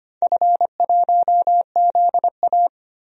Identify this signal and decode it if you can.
Morse code, heard as a keyed tone, about 25 wpm, 700 Hz. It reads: F1ZA
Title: DXCC-100 Call Signs - 4 Characters - 25wpm